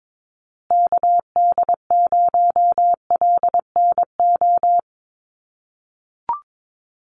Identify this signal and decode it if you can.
Morse code, heard as a keyed tone, about 22 wpm, 700 Hz. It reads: KD0LNO
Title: USA Call Signs Repeated 3x 22wpm